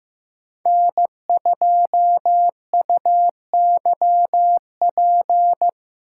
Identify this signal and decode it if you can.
Morse code, heard as a keyed tone, about 15 wpm, 700 Hz. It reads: N2UYP
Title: USA Call Signs - 5 Characters - 4x Character Spacing - 15wpm